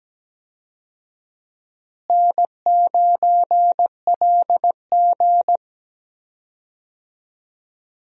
Morse code, heard as a keyed tone, about 17 wpm, 700 Hz. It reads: N9LG
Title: USA Call Signs - 4 Characters - Repeated 2x - 17wpm